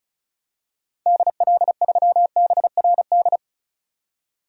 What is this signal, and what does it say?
Morse code, heard as a keyed tone, about 36 wpm, 700 Hz. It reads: DL3BRD
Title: DXCC-100 Call Signs - 6 Characters - 35wpm